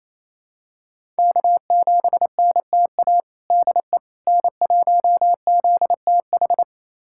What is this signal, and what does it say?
Morse code, heard as a keyed tone, about 28 wpm, 700 Hz. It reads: K7NTA DE N1ZTH
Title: USA Callsign DE Callsign - 5 Characters - 28wpm